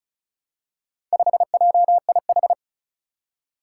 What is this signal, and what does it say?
Morse code, heard as a keyed tone, about 36 wpm, 700 Hz. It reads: 5JIH
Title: Four Challenging Letter-Number Combinations 35wpm